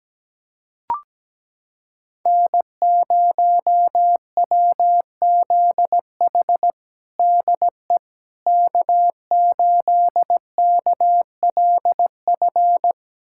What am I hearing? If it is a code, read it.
Morse code, heard as a keyed tone, about 17 wpm, 700 Hz. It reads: N0WZH DE K8KLF
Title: USA Callsign DE Callsign - 5 Characters - 17wpm